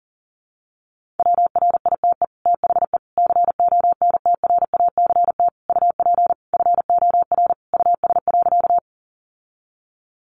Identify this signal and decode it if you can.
Morse code, heard as a keyed tone, about 40 wpm, 700 Hz. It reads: WRITE THE CONTRACT UP FOR US.
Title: Sentences from Top 1400 Words 40wpm